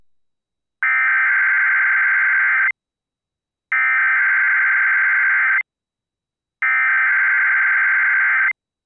Emergency Alert System (SAME) tones (header bursts), heard as harsh buzzing bursts. It is heard from 0.8 to 8.5 s.